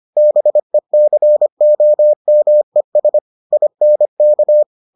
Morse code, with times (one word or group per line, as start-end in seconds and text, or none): 0.2-3.2 BECOMES
3.5-4.6 INK